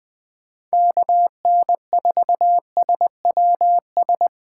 Morse code, heard as a keyed tone, about 20 wpm, 700 Hz. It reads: KN4SWS
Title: 2023 POTA Activator Call Signs - 20wpm